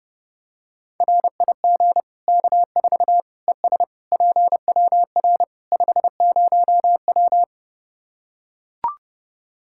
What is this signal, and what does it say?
Morse code, heard as a keyed tone, about 30 wpm, 700 Hz. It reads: RIG K4 ES PWR 50W
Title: Practice QSO - Exchange 2 - Info - 30wpm